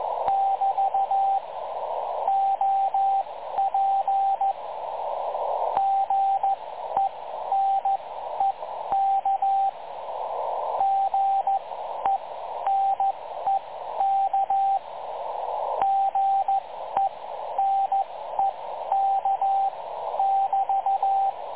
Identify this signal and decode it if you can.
Morse code, heard as a keyed tone, about 14 wpm, spaced as for 11 wpm, 780 Hz, with radio noise. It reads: <BT> OP GENEK GENEK GENEK <BT>
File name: dlazaawansowanych / nagraniazpasma / websdr recording 3565.0kHz